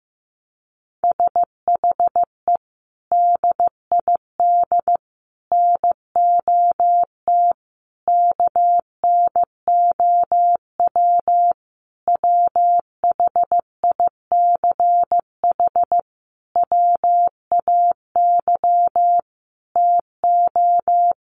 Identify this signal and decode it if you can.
Morse code, heard as a keyed tone, about 15 wpm, 700 Hz. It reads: SHE DID NOT KNOW WHICH WAY TO
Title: Sentences from Top 1100 Words 15wpm